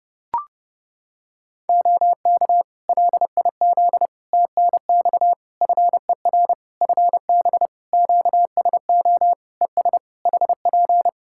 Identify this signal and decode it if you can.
Morse code, heard as a keyed tone, about 29 wpm, 700 Hz. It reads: OK LIZ TNX FER FB QSO ES HP
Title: K1USN SST QSO Practice 30wpm